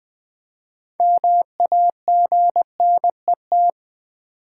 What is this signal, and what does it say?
Morse code, heard as a keyed tone, about 20 wpm, 700 Hz. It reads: MAGNET
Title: Build A Word - 3 Letters At A Time - 8-Letter Words - 20wpm